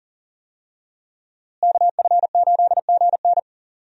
Morse code, heard as a keyed tone, about 40 wpm, 700 Hz. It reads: KF8GN
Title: USA Call Signs - 5 Characters - Repeated 2x - 40wpm